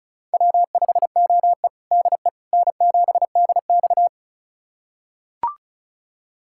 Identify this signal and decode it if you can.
Morse code, heard as a keyed tone, about 34 wpm, 700 Hz. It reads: W5OE DE N7DX
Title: USA Callsign DE Callsign - 4 Characters - 35wpm